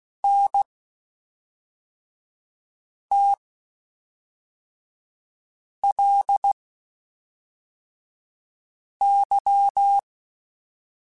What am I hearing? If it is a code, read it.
Morse code, heard as a keyed tone, about 16 wpm, 800 Hz. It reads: N T L Y